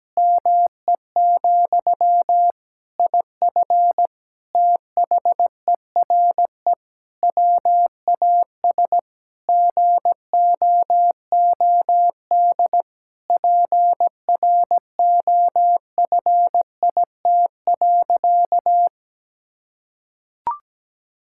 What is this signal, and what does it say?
Morse code, heard as a keyed tone, about 17 wpm, 700 Hz. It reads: ME, IF THERE WAS GOOD PROFIT.
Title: Sentences from Top 1600 Words 17wpm